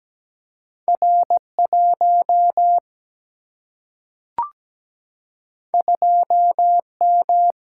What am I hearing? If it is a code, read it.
Morse code, heard as a keyed tone, about 17 wpm, 700 Hz. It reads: R1 2M
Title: Single-Letter Single-Number 17wpm